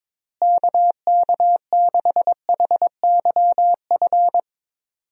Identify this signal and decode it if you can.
Morse code, heard as a keyed tone, about 22 wpm, 700 Hz. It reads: KK6HYF
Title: USA Call Signs - 6 Characters - Repeated 3x - 22wpm